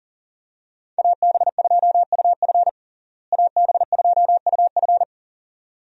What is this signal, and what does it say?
Morse code, heard as a keyed tone, about 40 wpm, 700 Hz. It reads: AB2UF AB2UF
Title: USA Call Signs - 5 Characters - Repeated 2x - 40wpm